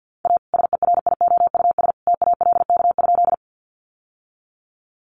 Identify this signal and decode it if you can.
Morse code, heard as a keyed tone, about 48 wpm, 700 Hz. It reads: A SERIOUS TALK?
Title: Sentences from Top 1100 Words 50wpm